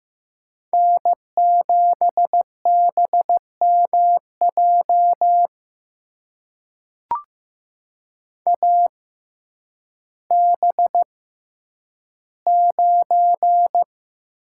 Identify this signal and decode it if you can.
Morse code, heard as a keyed tone, about 15 wpm, spaced as for 9 wpm, 700 Hz. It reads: N7BMJ AB9
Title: USA Call Signs - 5 Characters - 6x Character Spacing - 15wpm